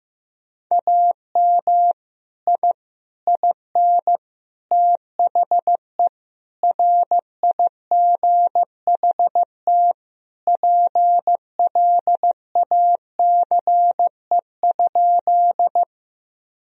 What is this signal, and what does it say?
Morse code, heard as a keyed tone, about 15 wpm, 700 Hz. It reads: AM I IN THE RIGHT PLACE?
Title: Sentences from Top 1000 Words - Review Part 2 - 15wpm